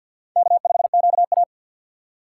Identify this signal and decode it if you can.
Morse code, heard as a keyed tone, about 48 wpm, 700 Hz. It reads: K5QA
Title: USA Call Signs - 4 Characters - 3x Character Spacing - 50wpm